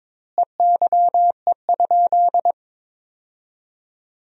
Morse code, heard as a keyed tone, about 22 wpm, 700 Hz. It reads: EYE?
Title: 2x Word-Spacing - Sentences from Top 300 Words - 22wpm